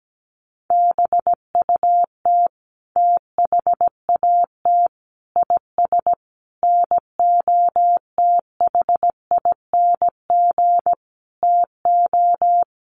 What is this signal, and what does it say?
Morse code, heard as a keyed tone, about 17 wpm, 700 Hz. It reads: BUT THAT IS NOTHING TO
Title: Sentences from Top 500 Words 17wpm